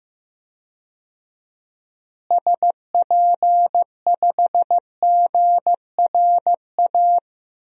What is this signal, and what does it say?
Morse code, heard as a keyed tone, about 15 wpm, 700 Hz. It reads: SP5GRA
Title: DXCC-100 Call Signs - 6 Characters - Repeated 2x - 15wpm